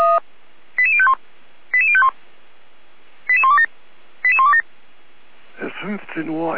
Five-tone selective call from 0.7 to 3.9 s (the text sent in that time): 88051 88051 88127